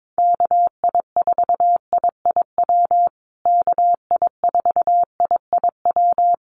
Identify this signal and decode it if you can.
Morse code, heard as a keyed tone, about 22 wpm, 700 Hz. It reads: KI4IIW KI4IIW
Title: USA Call Signs Repeated 3x 22wpm